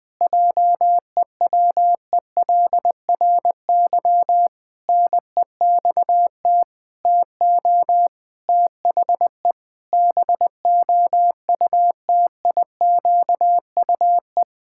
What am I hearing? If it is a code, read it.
Morse code, heard as a keyed tone, about 20 wpm, 700 Hz. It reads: JEWELRY NEXT TO THE BOUTIQUE
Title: Pangram Challenge v2 - 20wpm